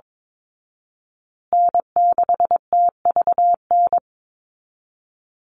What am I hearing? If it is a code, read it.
Morse code, heard as a keyed tone, about 22 wpm, 700 Hz. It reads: N6TVN
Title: Call Signs Repeated 2x - 22wpm